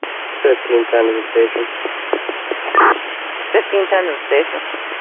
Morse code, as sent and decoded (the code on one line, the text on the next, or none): none